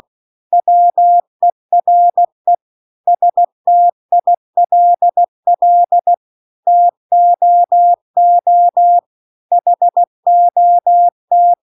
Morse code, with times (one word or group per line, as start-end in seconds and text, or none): 0.5-2.5 WERE
3.1-6.1 STILL
6.7-9.0 TOO
9.5-11.5 HOT